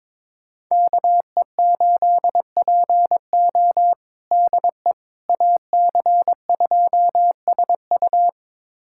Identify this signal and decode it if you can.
Morse code, heard as a keyed tone, about 21 wpm, 700 Hz. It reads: KE8PO DE AC2SU
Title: USA Callsign DE Callsign - 5 Characters - 22wpm